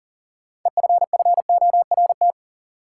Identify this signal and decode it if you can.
Morse code, heard as a keyed tone, about 40 wpm, 700 Hz. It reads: EFFORT
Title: Sets of 4 Words - New Word Drill - 40wpm